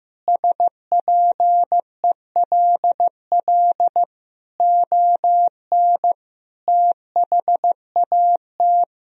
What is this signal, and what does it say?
Morse code, heard as a keyed tone, about 15 wpm, 700 Hz. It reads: SPELL ON THAT
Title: Sentences from Top 1000 Words - Review Part 1 - 15wpm